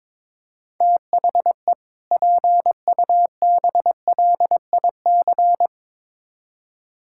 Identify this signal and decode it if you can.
Morse code, heard as a keyed tone, about 22 wpm, 700 Hz. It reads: THE PUBLIC
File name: Build A Phrase - 2-Word Phrases - 22wpm